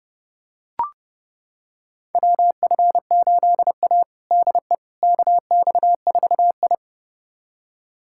Morse code, heard as a keyed tone, about 28 wpm, 700 Hz. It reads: WF8A DE KX4I
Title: USA Callsign DE Callsign - 4 Characters - 30wpm